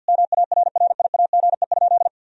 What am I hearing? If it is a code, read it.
Morse code, heard as a keyed tone, about 50 wpm, 700 Hz. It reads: MARRIAGE?
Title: Sentences from Top 1000 Words - Review Part 4 - 50wpm